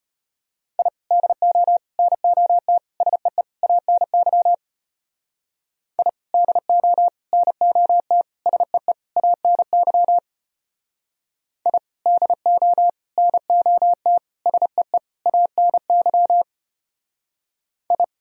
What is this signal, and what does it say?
Morse code, heard as a keyed tone, about 33 wpm, 700 Hz. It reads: I DO NOT SEE ANY I DO NOT SEE ANY I DO NOT SEE ANY I